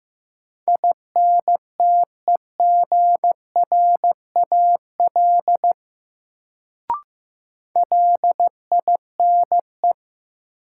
Morse code, heard as a keyed tone, about 15 wpm, 700 Hz. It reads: INTEGRAL LINE
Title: Sets of 5 Words - New Word Drill - 15wpm